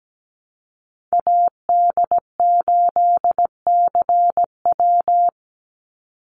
Morse code, heard as a keyed tone, about 17 wpm, 700 Hz. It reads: AD8CW
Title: Call Signs - 17wpm